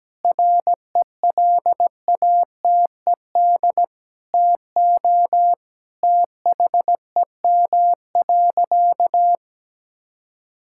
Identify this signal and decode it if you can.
Morse code, heard as a keyed tone, about 17 wpm, 700 Hz. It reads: RELATED TO THEM.
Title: Sentences from Top 1000 Words - Review Part 5 - 17wpm